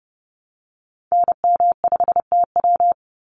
Morse code, heard as a keyed tone, about 30 wpm, 700 Hz. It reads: NM5TW
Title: NA SOTA Activator Callsigns 30wpm